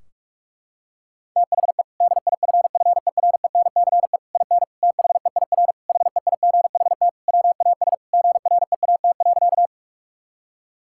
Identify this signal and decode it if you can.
Morse code, heard as a keyed tone, about 46 wpm, 700 Hz. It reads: THE DIFFERENCE IN THEIR HEIGHT WAS GREAT.
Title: Sentences from Top 1600 Words 45wpm